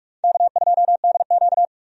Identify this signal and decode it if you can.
Morse code, heard as a keyed tone, about 46 wpm, 700 Hz. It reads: K2DQ
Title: USA Call Signs - 4 Characters - 45wpm